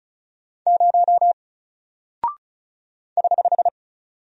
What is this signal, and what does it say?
Morse code, heard as a keyed tone, about 36 wpm, 700 Hz. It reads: 0 <HH>
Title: Single Character (Letter:Number:Punctuation:Prosign) Alt 35wpm